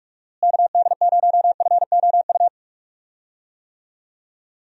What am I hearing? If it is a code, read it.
Morse code, heard as a keyed tone, about 44 wpm, 700 Hz. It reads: KD0FOU
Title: USA Call Signs - 6 Characters - Repeated 2x - 45wpm